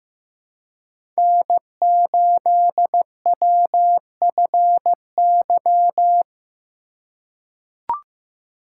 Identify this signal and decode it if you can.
Morse code, heard as a keyed tone, about 15 wpm, 700 Hz. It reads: N8WFY
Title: USA Call Signs - 5 Characters - 4x Character Spacing - 15wpm